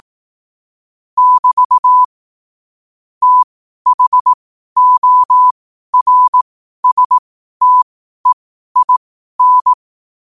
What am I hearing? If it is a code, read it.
Morse code, heard as a keyed tone, about 18 wpm, spaced as for 12 wpm, 1000 Hz. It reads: <BT> THORSTEIN